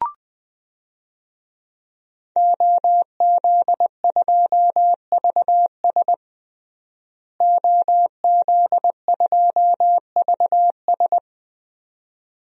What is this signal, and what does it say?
Morse code, heard as a keyed tone, about 21 wpm, 700 Hz. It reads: OZ2VS OZ2VS
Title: DXCC-100 Call Signs - 5 Characters - Repeated 2x - 20wpm